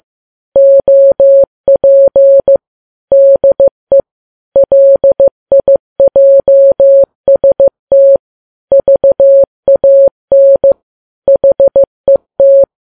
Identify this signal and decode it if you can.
Morse code, heard as a keyed tone, about 15 wpm, 550 Hz. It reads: OP DE LIJST VAN HET
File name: Files Web tls / Morse on9fv / morse15kps